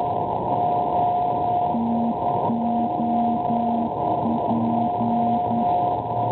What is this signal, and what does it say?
Morse code, heard as a keyed tone, about 9 wpm, 230 Hz. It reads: TOP
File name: NDB Torino-Poirino Italia 392.5 kHz